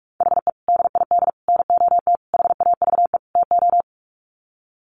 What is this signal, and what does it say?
Morse code, heard as a keyed tone, about 44 wpm, 700 Hz. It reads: HE DID NOT HAVE TO
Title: Sets of 5 Words 45wpm